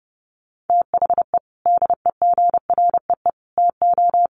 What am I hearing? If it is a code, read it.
Morse code, heard as a keyed tone, about 30 wpm, 700 Hz. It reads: THE DEGREE TO
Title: Sets of 5 Words Encore2 30wpm